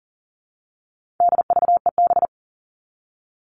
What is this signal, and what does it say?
Morse code, heard as a keyed tone, about 40 wpm, 700 Hz. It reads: DVEB
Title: All Cut Numbers - 4-Digit 40wpm